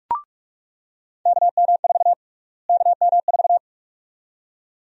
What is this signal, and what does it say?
Morse code, heard as a keyed tone, about 42 wpm, 700 Hz. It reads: KM4 KM4
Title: USA Call Sign Prefixes - Repeated 2x - 45wpm